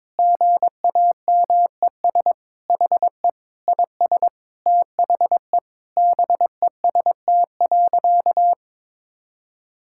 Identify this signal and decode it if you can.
Morse code, heard as a keyed tone, about 22 wpm, 700 Hz. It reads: GAMES HE IS THE BEST.
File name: practice / 22wpm / Sentences with 3-Word Phrases - 22wpm